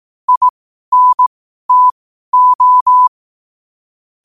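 Morse code, heard as a keyed tone, about 18 wpm, spaced as for 12 wpm, 1000 Hz. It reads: INTO
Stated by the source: Morse at 12 wpm.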